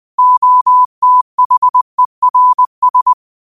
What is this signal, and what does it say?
Morse code, heard as a keyed tone, about 20 wpm, 1000 Hz. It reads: OTHERS
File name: qotdfiles / 20wpm